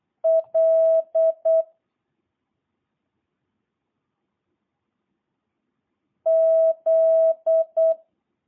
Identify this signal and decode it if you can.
Morse code, heard as a keyed tone, about 8 wpm, 650 Hz. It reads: L Z